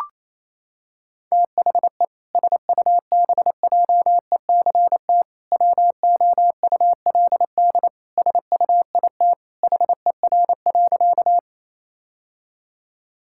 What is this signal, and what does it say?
Morse code, heard as a keyed tone, about 28 wpm, 700 Hz. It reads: THE SUBJECT WOULD SUIT HER.
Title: Sentences from Top 1600 Words 28wpm